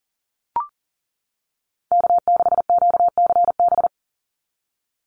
Morse code, heard as a keyed tone, about 40 wpm, 700 Hz. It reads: K6QCB
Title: NA SOTA Chaser Callsigns 40wpm